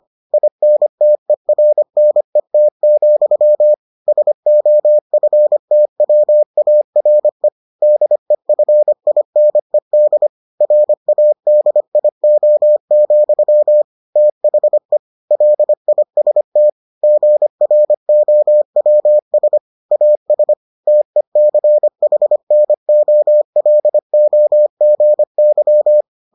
Morse code, heard as a keyed tone, about 25 wpm, 600 Hz. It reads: INTERNET, SOFTWARE DEFINED RADIO, THE LIST GROWS AS TECHNOLOGY